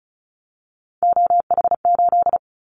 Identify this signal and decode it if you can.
Morse code, heard as a keyed tone, about 34 wpm, 700 Hz. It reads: OH8
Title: Callsign Prefixes for Top 25 DXCC - Repeated 3x - 35wpm